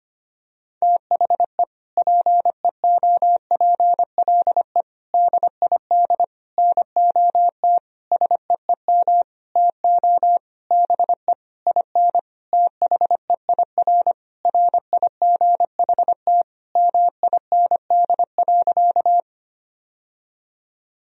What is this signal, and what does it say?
Morse code, heard as a keyed tone, about 25 wpm, 700 Hz. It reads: THE PEOPLE DID NOT SEEM TO BE IN THEIR RIGHT MIND.
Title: Sentences from Top 1000 Words - Review Part 1 - 25wpm